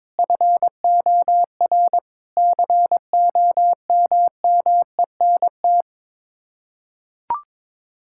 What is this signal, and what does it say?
Morse code, heard as a keyed tone, about 22 wpm, 700 Hz. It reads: FOR COMMENT
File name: practice / 22wpm / Sets of 6 Words - 22wpm